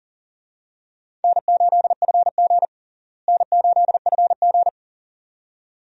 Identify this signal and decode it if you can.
Morse code, heard as a keyed tone, about 40 wpm, 700 Hz. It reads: N8FG N8FG
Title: USA Call Signs - 4 Characters - Repeated 2x - 40wpm